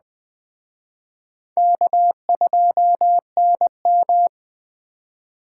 Morse code, heard as a keyed tone, about 20 wpm, 700 Hz. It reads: K2NM
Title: Speed Racing - USA 1x2 Call Signs 17wpm